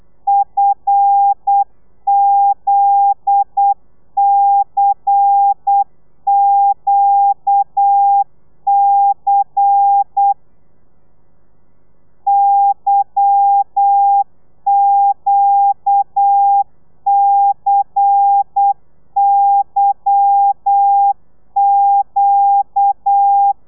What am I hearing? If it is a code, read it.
Morse code, heard as a keyed tone, about 8 wpm, 800 Hz. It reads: FZCQC YQCYQ
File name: telegrafia / i2tfj / Lez10 35